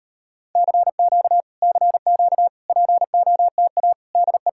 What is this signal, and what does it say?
Morse code, heard as a keyed tone, about 38 wpm, 700 Hz. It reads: CQ CQ POTA DE